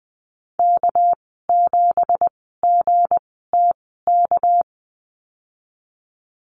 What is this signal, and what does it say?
Morse code, heard as a keyed tone, about 20 wpm, spaced as for 15 wpm, 700 Hz. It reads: K7GTK
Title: 207 Lesson 33g - Letter j - Call Signs 20wpm